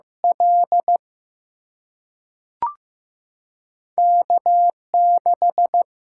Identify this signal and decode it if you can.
Morse code, heard as a keyed tone, about 15 wpm, 700 Hz. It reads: L K 6
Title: Single-Letter Single-Number 15wpm